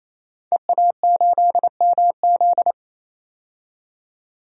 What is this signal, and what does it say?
Morse code, heard as a keyed tone, about 28 wpm, 700 Hz. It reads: EA8MZ
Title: DXCC-100 Call Signs - 5 Characters - 28wpm